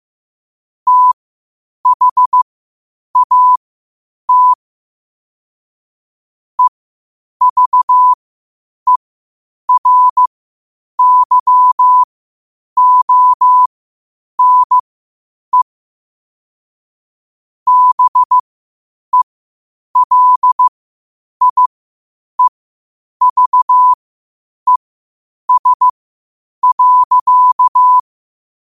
Morse code, heard as a keyed tone, about 15 wpm, spaced as for 8 wpm, 1000 Hz. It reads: THAT EVERYONE BELIEVES.
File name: qotdfiles / 8wpm